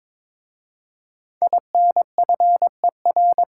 Morse code, heard as a keyed tone, about 22 wpm, 700 Hz. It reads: INFER
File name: 5 Letter Words - First 3 Letters Given - 22wpm